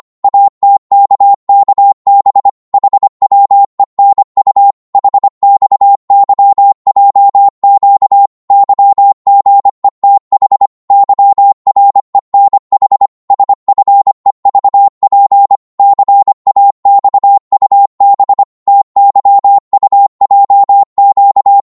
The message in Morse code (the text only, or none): ATKKB HWENU HXYJQ YGETH YRENH SFEVP CAXUB TYUJQ